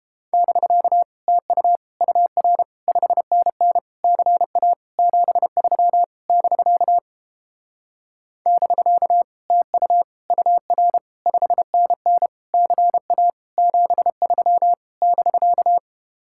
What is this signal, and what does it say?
Morse code, heard as a keyed tone, about 31 wpm, 700 Hz. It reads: <BK> TU UR 5NN CA 73 <BK> <BK> TU UR 5NN CA 73 <BK>